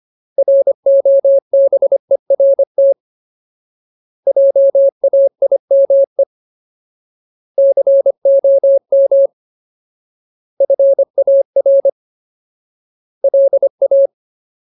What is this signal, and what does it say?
Morse code, heard as a keyed tone, about 25 wpm, 550 Hz. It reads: ROBERT JAIME COM FAR LA
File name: Ultimate Word List / 25WPM Ultimate